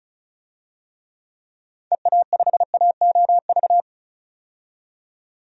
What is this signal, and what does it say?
Morse code, heard as a keyed tone, about 36 wpm, 700 Hz. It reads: EA5AOV
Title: DXCC-100 Call Signs - 6 Characters - 35wpm